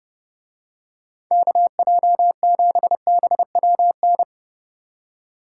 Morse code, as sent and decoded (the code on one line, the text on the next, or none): -.- .--- --... -... .-- -.
KJ7BWN